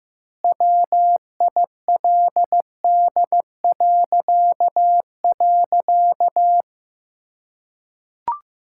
Morse code, heard as a keyed tone, about 16 wpm, 700 Hz. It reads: WILD..
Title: Pangram Challenge v2 - 15wpm